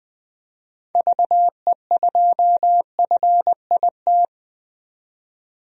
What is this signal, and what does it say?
Morse code, heard as a keyed tone, about 20 wpm, 700 Hz. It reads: VE2FIT
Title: DXCC-100 Call Signs - 6 Characters - 20wpm